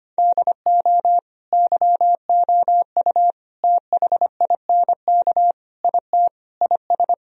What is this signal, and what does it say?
Morse code, heard as a keyed tone, about 25 wpm, 700 Hz. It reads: DO YOU THINK IT IS
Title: Sentences from Top 1000 Words - Review Part 1 - 25wpm